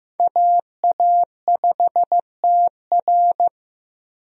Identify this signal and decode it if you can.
Morse code, heard as a keyed tone, about 15 wpm, 700 Hz. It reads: AA5TR
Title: USA Call Signs - 5 Characters - 15wpm